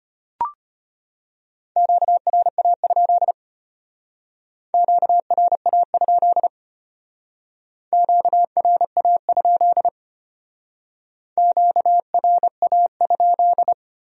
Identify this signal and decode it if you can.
Morse code, heard as a keyed tone, about 32 wpm, 700 Hz. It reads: QRA? QRA? QRA? QRA?